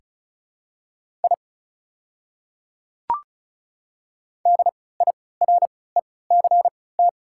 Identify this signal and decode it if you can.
Morse code, heard as a keyed tone, about 36 wpm, spaced as for 20 wpm, 700 Hz. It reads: I DIRECT
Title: 3x Character Spacing - Top 500 Words - 35wpm